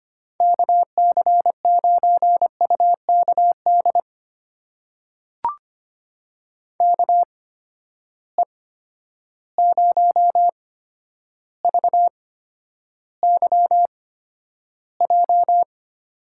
Morse code, heard as a keyed tone, about 25 wpm, spaced as for 11 wpm, 700 Hz. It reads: KC9UKD KE0VYJ
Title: USA Call Signs - 6 Characters - 8x Character Spacing - 25wpm